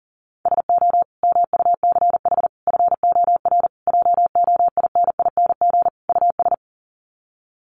Morse code, heard as a keyed tone, about 40 wpm, 700 Hz. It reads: SO MUCH FOR JOINING US
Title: Sets of 5 Words 40wpm